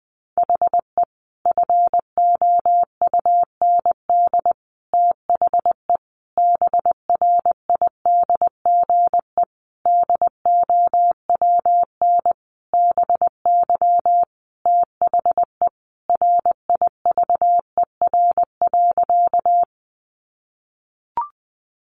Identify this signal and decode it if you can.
Morse code, heard as a keyed tone, about 20 wpm, 700 Hz. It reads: HE FOUND THE BRIDGE DOWN BY THE RIVER.
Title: Sentences from Top 1200 Words 20wpm